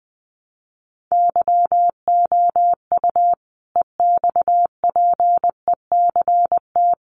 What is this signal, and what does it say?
Morse code, heard as a keyed tone, about 20 wpm, 700 Hz. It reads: YOU EXPECT
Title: Sentences from Top 1100 Words 20wpm